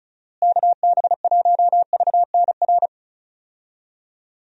Morse code, heard as a keyed tone, about 36 wpm, 700 Hz. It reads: KB1VNR
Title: USA Call Signs - 6 Characters - Repeated 3x - 35wpm